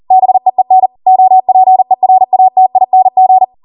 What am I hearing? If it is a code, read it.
Morse code, heard as a keyed tone, about 40 wpm, 750 Hz. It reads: BEEN OPERATING
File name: Morse / 40WPM